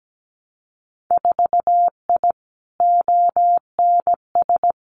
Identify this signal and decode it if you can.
Morse code, heard as a keyed tone, about 17 wpm, 700 Hz. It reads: 4I ONS
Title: Winter Field Day Exchanges Encore 17wpm